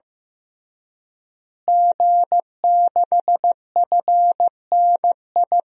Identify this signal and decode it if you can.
Morse code, heard as a keyed tone, about 15 wpm, 700 Hz. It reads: G6FNI
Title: DXCC-100 Call Signs - 5 Characters - Repeated 3x - 15wpm